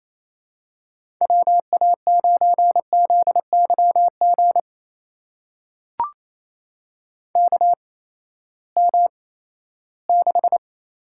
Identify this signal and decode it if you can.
Morse code, heard as a keyed tone, about 28 wpm, spaced as for 16 wpm, 700 Hz. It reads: WA9ZYG KM6
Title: USA Call Signs - 6 Characters - 8x Character Spacing - 28wpm